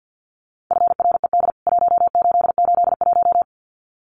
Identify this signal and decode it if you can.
Morse code, heard as a keyed tone, about 48 wpm, 700 Hz. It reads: FRED 1881
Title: CWT Exchange 50wpm